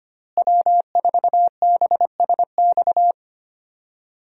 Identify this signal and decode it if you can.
Morse code, heard as a keyed tone, about 25 wpm, 700 Hz. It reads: W4BSX
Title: USA Call Signs - 5 Characters - Repeated 2x - 25wpm